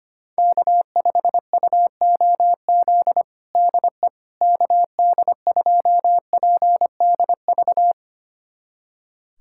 Morse code, heard as a keyed tone, about 25 wpm, 700 Hz. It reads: K5UOZ DE KD2PDV
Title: USA Callsign DE Callsign 25wpm